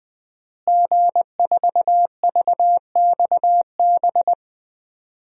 Morse code, heard as a keyed tone, about 20 wpm, 700 Hz. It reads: G4VXB
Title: DXCC-100 Call Signs - 5 Characters - Repeated 2x - 20wpm